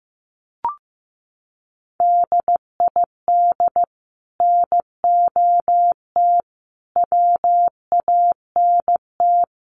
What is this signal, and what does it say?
Morse code, heard as a keyed tone, about 15 wpm, 700 Hz. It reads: DID NOT WANT
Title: Sets of 5 Words 15wpm